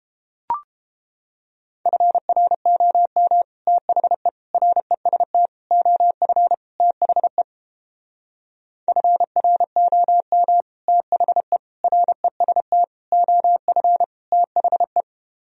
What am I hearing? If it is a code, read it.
Morse code, heard as a keyed tone, about 29 wpm, 700 Hz. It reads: FROM THE REST OF THE FROM THE REST OF THE